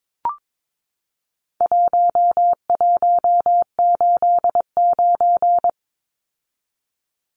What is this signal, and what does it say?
Morse code, heard as a keyed tone, about 22 wpm, 700 Hz. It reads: 1189
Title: Numbers - 4-Digit 22wpm